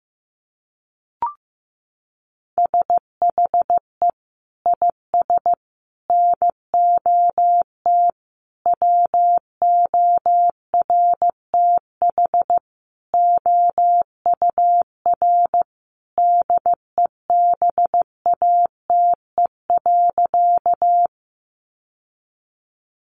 Morse code, heard as a keyed tone, about 15 wpm, 700 Hz. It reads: SHE IS NOT WORTH OUR DEBATE.
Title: Sentences from Top 1100 Words 15wpm